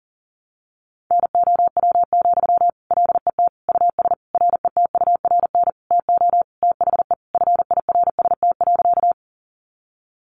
Morse code, heard as a keyed tone, about 40 wpm, 700 Hz. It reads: NOW, LET US RETURN TO THE FIRST.
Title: Sentences from Top 800 Words 40wpm